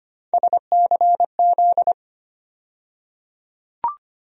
Morse code, heard as a keyed tone, about 25 wpm, 700 Hz. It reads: SCZ
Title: USA Call Sign Postfixes - 3 Letters 25wpm